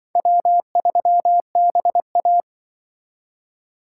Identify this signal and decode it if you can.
Morse code, heard as a keyed tone, about 25 wpm, 700 Hz. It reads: W3BA
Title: Speed Racing - USA 1x2 Call Signs 20wpm